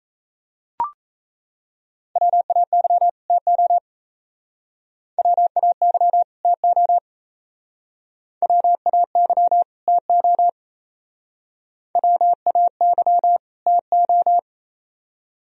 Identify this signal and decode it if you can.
Morse code, heard as a keyed tone, about 35 wpm, 700 Hz. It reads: WAY TO WAY TO WAY TO WAY TO
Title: Speed-Racing - Sets of 2 Words 28wpm